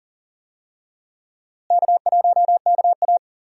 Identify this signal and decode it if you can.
Morse code, heard as a keyed tone, about 40 wpm, 700 Hz. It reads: K1KA
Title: USA Call Signs - 4 Characters - Repeated 2x - 40wpm